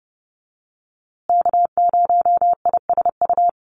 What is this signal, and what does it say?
Morse code, heard as a keyed tone, about 30 wpm, 700 Hz. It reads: K0ISU
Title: USA Call Signs Repeated 2x 30wpm